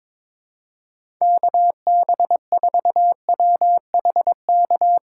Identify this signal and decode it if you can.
Morse code, heard as a keyed tone, about 22 wpm, 700 Hz. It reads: KB4WHK
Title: USA Call Signs - 6 Characters - 4x Character Spacing - 22wpm